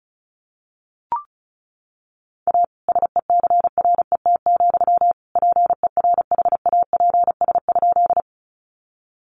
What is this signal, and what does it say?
Morse code, heard as a keyed tone, about 34 wpm, 700 Hz. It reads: A SECRET, PERHAPS?
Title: Sentences from Top 1200 Words 35wpm